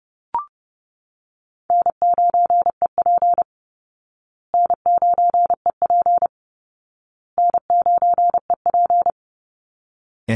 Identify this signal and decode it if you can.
Morse code, heard as a keyed tone, about 28 wpm, 700 Hz. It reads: N9EP N9EP N9EP
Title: Easy Call Signs Repeated 3x - 30wpm